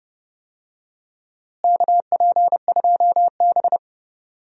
Morse code, heard as a keyed tone, about 31 wpm, 700 Hz. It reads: KP2B
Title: DXCC-100 Call Signs - 4 Characters - Repeated 3x - 30wpm